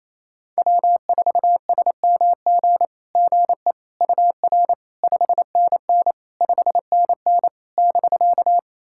Morse code, heard as a keyed tone, about 28 wpm, 700 Hz. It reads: W4SMG GE UR 5NN 5NN <BK>
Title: SOTA QSO Practice 28wpm